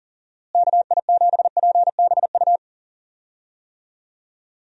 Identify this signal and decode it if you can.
Morse code, heard as a keyed tone, about 40 wpm, 700 Hz. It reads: KI7PBU
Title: USA Call Signs - 6 Characters - Repeated 3x - 40wpm